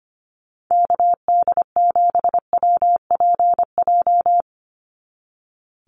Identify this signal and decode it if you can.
Morse code, heard as a keyed tone, about 25 wpm, 700 Hz. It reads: KD7WPJ
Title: NA SOTA Activator Callsigns 25wpm